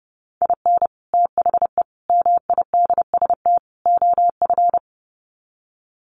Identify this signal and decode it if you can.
Morse code, heard as a keyed tone, about 30 wpm, 700 Hz. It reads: IN THE MIDST OF
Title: Sets of 4 Words 30wpm